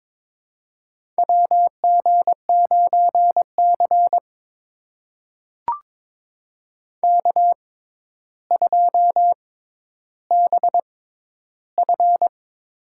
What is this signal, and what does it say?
Morse code, heard as a keyed tone, about 22 wpm, spaced as for 11 wpm, 700 Hz. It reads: WG9C K2BF
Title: USA Call Signs - 4 Characters - 6x Character Spacing - 22wpm